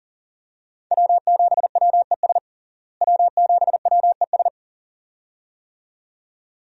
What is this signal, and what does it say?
Morse code, heard as a keyed tone, about 40 wpm, 700 Hz. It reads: W7WES W7WES
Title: USA Call Signs - 5 Characters - Repeated 2x - 40wpm